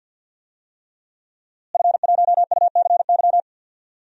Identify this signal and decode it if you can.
Morse code, heard as a keyed tone, about 50 wpm, 700 Hz. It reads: F1UCY
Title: DXCC-100 Call Signs - 5 Characters - Repeated 3x - 50wpm